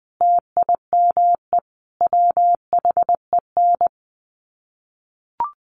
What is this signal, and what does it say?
Morse code, heard as a keyed tone, about 19 wpm, 700 Hz. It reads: TIME WHEN
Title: Sets of 5 Words 20wpm